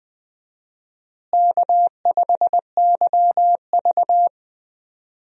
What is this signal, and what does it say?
Morse code, heard as a keyed tone, about 21 wpm, 700 Hz. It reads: K5YV
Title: USA Call Signs - 4 Characters - Repeated 2x - 20wpm